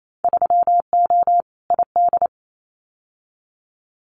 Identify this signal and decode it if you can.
Morse code, heard as a keyed tone, about 28 wpm, 700 Hz. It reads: 3O ID
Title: Winter Field Day Exchanges Encore 28wpm